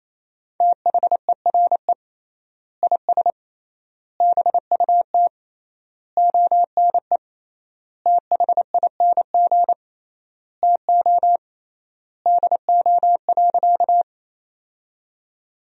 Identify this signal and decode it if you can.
Morse code, heard as a keyed tone, about 28 wpm, 700 Hz. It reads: THERE IS BUT ONE THING TO DO.
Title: 3x Word-Spacing - Sentences from Top 100 Words - 28wpm